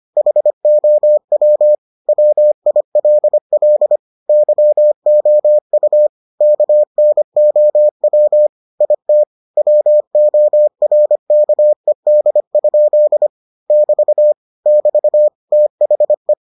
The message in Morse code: HOW WILL YOU KNOW IT WORKED? <BT> <BT> THE

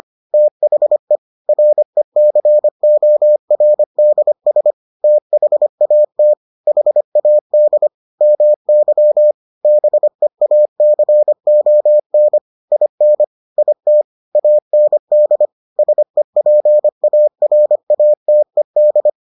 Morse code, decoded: THE RECORDS THAT HAD MY BEACON IN IT AND SEPARATED

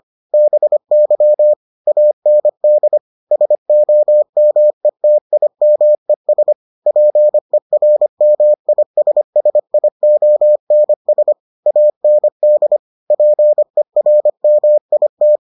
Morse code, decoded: BY AND SOMETIMES PERMISSIONS AND PERMIT

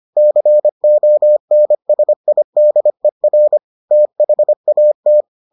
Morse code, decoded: CONSIDER THAT

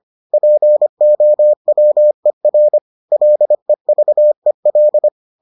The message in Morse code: POWER LEVEL

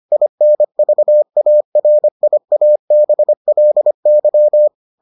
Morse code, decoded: INVARIABLY

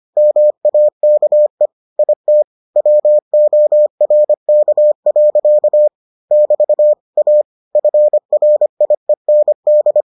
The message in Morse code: MAKE IT WORK. <BT> A FRIEND

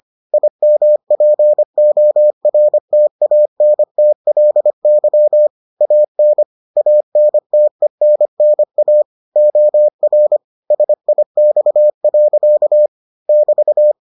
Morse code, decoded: IMPORTANTLY AN ANTENNA OR SIX. <BT>

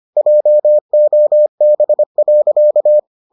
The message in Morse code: JOB.

